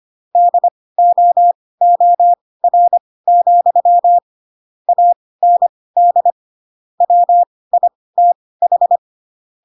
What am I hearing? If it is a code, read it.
Morse code, heard as a keyed tone, about 25 wpm, spaced as for 18 wpm, 700 Hz. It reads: DOOR, AND WITH
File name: Pinocchio ch3 12 25 ik1hge